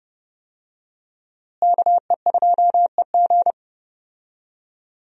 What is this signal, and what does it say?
Morse code, heard as a keyed tone, about 30 wpm, 700 Hz. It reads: KE2EG
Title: USA Call Signs - 5 Characters - Repeated 2x - 30wpm